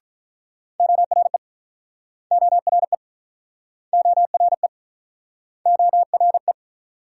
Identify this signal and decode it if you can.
Morse code, heard as a keyed tone, about 48 wpm, 700 Hz. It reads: ORE ORE ORE ORE